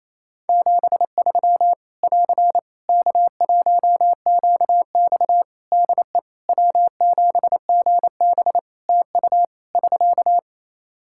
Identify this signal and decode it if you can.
Morse code, heard as a keyed tone, about 28 wpm, 700 Hz. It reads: 73 <AR> K1QX DE W7GB TU <SK>
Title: K1USN SST QSO Practice 28wpm